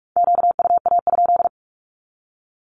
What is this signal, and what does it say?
Morse code, heard as a keyed tone, about 46 wpm, 700 Hz. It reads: QUA?
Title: Q-Codes 45wpm